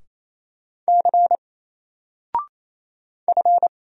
Morse code, heard as a keyed tone, about 27 wpm, 700 Hz. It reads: C F
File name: Single-Letter-Number-28wpm